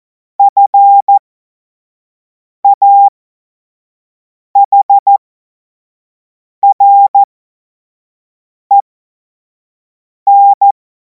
Morse code, heard as a keyed tone, about 14 wpm, spaced as for 5 wpm, 800 Hz. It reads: FAHREN
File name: Morsekurs / Witze / Witz 24. März 2026 25 F70